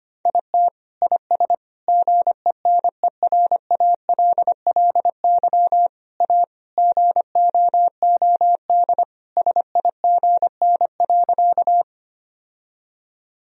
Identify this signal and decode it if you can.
Morse code, heard as a keyed tone, about 25 wpm, 700 Hz. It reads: IT IS GENERALLY A GOOD SIGN.
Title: Sentences from Top 1000 Words - Review Part 4 - 25wpm